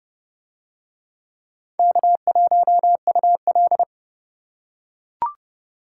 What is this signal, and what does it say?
Morse code, heard as a keyed tone, about 27 wpm, 700 Hz. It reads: K1UL